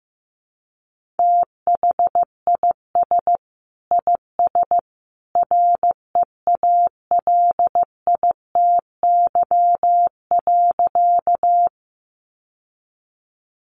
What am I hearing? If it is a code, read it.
Morse code, heard as a keyed tone, about 15 wpm, 700 Hz. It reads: THIS IS REALITY.
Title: Sentences from Top 1200 Words 15wpm